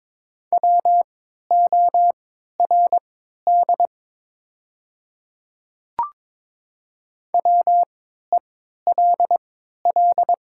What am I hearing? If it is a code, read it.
Morse code, heard as a keyed tone, about 22 wpm, spaced as for 13 wpm, 700 Hz. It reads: WORD WELL
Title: 3x Character Spacing - Top 500 Words - 22wpm